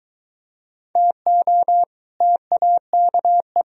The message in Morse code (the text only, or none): TO TAKE